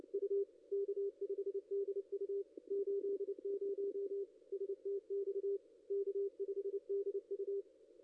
Morse code, heard as a keyed tone, about 28 wpm, 410 Hz. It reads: U K5DU 80 STX K5DU
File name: ss-cw-k5tr-op-n5rz